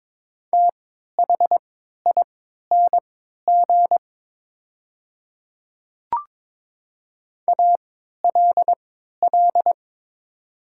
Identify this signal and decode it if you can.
Morse code, heard as a keyed tone, about 22 wpm, spaced as for 13 wpm, 700 Hz. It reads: THING ALL
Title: 3x Character Spacing - Top 100 Words - 22wpm